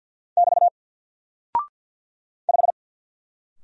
Morse code, heard as a keyed tone, about 49 wpm, 700 Hz. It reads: <BT> 5
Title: Single Character (Letter:Number:Punctuation:Prosign) Alt 50wpm